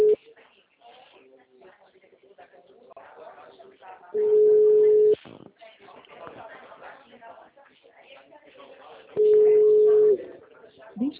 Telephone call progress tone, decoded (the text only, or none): ringback tone